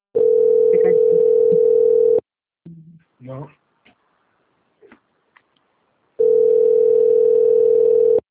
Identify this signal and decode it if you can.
Ringback tone (North America), 440 + 480 Hz, with speech.
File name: out- - - - - .40977